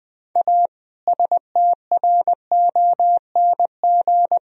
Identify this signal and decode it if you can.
Morse code, heard as a keyed tone, about 20 wpm, 700 Hz. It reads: A STRONG
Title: Sentences from Top 1000 Words - Review Part 3 - 20wpm